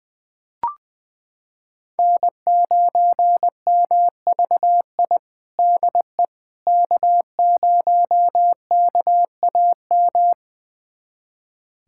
Morse code, heard as a keyed tone, about 20 wpm, 700 Hz. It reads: N9MVI DE K0KAM
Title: USA Callsign DE Callsign - 5 Characters - 20wpm